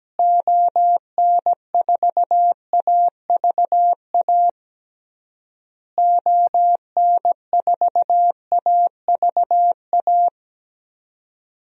Morse code, heard as a keyed tone, about 17 wpm, 700 Hz. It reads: ON4AVA ON4AVA
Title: DXCC-100 Call Signs - 6 Characters - Repeated 3x - 17wpm